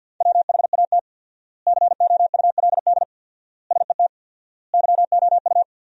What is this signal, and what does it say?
Morse code, heard as a keyed tone, about 50 wpm, 700 Hz. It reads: WHAT COULD SET YOU
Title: 4x Word-Spacing - Sentences from Top 200 Words - 50wpm